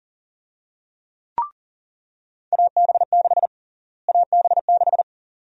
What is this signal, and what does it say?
Morse code, heard as a keyed tone, about 40 wpm, 700 Hz. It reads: AB6 AB6
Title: USA Call Sign Prefixes - Repeated 2x - 40wpm